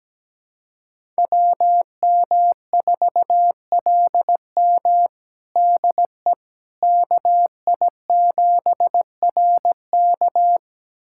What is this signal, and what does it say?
Morse code, heard as a keyed tone, about 17 wpm, 700 Hz. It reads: WM4LM DE KI7RK
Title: USA Callsign DE Callsign - 5 Characters - 17wpm